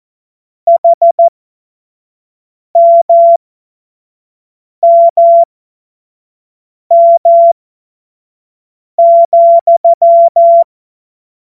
Morse code, heard as a keyed tone, about 14 wpm, spaced as for 5 wpm, 680 Hz. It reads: HMMM,
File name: Morsekurs / Witze / Witz 05 März 2024 25 F70